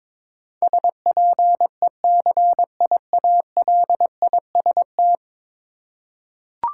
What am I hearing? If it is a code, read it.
Morse code, heard as a keyed tone, about 22 wpm, 700 Hz. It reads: SPECIALIST
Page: morse-code-ninja-practice-22wpm